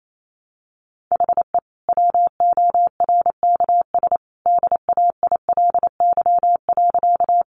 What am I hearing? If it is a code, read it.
Morse code, heard as a keyed tone, about 28 wpm, 700 Hz. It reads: HE WORKS DAILY.
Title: Sentences from Top 1100 Words 28wpm